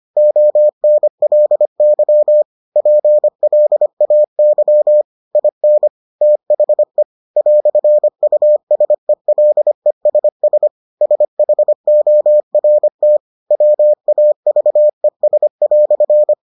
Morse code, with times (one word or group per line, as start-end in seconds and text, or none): 0.2-2.4 ONLY
2.8-5.0 PLAY
5.4-5.9 IN
6.2-7.0 THE
7.4-10.7 "USELESS
11.0-13.2 SHORT
13.5-16.3 WAVES"